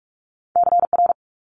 Morse code, heard as a keyed tone, about 46 wpm, 700 Hz. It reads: CR
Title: Multi-letter Phonemes Quick Fire 45wpm